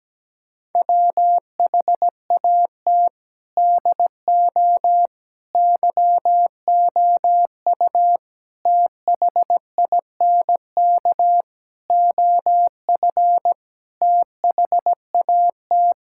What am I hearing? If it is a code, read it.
Morse code, heard as a keyed tone, about 17 wpm, 700 Hz. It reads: WHAT DO YOU THINK OF THAT
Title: Sentences from Top 1000 Words - Review Part 6 - 17wpm